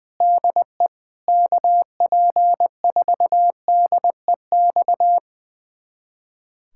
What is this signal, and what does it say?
Morse code, heard as a keyed tone, about 20 wpm, 700 Hz. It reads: DE KP4DEX
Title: USA Callsign DE Callsign - 6 Characters - 20wpm